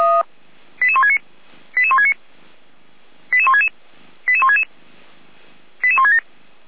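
Five-tone selective call (ZVEI), heard as quick runs of five tones, tones of about 70 ms. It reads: 88169 88169 88166 88166 88167
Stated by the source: ZVEI selective call received by SDR, recorded at feuerwehr-orlamuende.org fire brigade.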